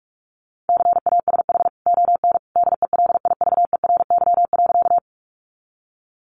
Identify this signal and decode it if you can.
Morse code, heard as a keyed tone, about 44 wpm, 700 Hz. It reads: CASH ON DELIVERY.
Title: Sentences from Top 1500 Words 45wpm